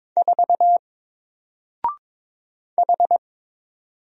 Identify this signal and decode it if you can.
Morse code, heard as a keyed tone, about 22 wpm, 700 Hz. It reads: 4 H
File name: Single-Letter-Number-22wpm